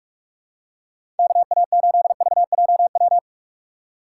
Morse code, heard as a keyed tone, about 42 wpm, 700 Hz. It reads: KA8VJW
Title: USA Call Signs - 6 Characters - 3x Character Spacing - 45wpm